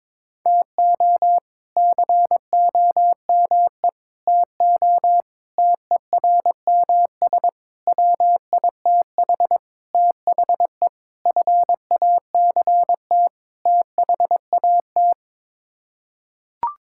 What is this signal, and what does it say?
Morse code, heard as a keyed tone, about 22 wpm, 700 Hz. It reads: TO COME TO TERMS WITH THE FACT THAT
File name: practice / 22wpm / Sets of 8 Words - 22wpm